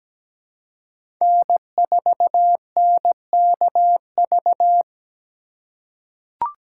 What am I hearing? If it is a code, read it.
Morse code, heard as a keyed tone, about 17 wpm, 700 Hz. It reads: N4NKV
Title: USA Call Signs - 4x Character Spacing - 17wpm